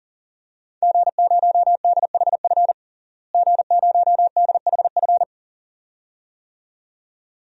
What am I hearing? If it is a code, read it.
Morse code, heard as a keyed tone, about 40 wpm, 700 Hz. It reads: G0DHF G0DHF
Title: DXCC-100 Call Signs - 5 Characters - Repeated 2x - 40wpm